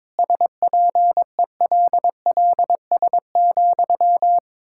Morse code, heard as a keyed tone, about 22 wpm, 700 Hz. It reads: SPELLS,
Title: Pangram Challenge v2 - 22wpm